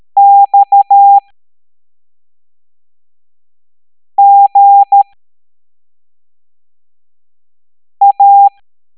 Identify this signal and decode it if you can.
Morse code, heard as a keyed tone, about 13 wpm, spaced as for 3 wpm, 800 Hz. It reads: XGA